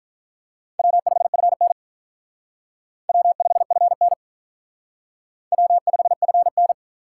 Morse code, heard as a keyed tone, about 45 wpm, 700 Hz. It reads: W5FN W5FN W5FN